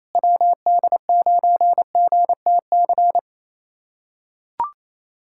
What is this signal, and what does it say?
Morse code, heard as a keyed tone, about 28 wpm, 700 Hz. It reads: WD9GTC
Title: USA Callsign DE Callsign 28wpm